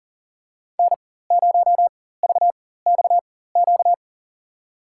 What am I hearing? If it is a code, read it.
Morse code, heard as a keyed tone, about 40 wpm, spaced as for 19 wpm, 700 Hz. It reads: N0VXQ
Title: USA Call Signs - 5 Characters - 4x Character Spacing - 40wpm